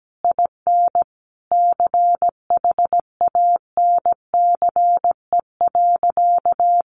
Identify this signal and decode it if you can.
Morse code, heard as a keyed tone, about 17 wpm, 700 Hz. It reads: IN CHANCE.
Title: Sentences from Top 1000 Words 17wpm